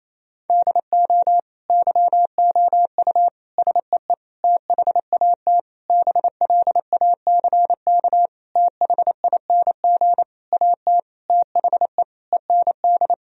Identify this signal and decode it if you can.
Morse code, heard as a keyed tone, about 28 wpm, 700 Hz. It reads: DO YOU SEE THAT BLACK THING AT THE END
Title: Sentences from Top 1000 Words - Review Part 2 - 28wpm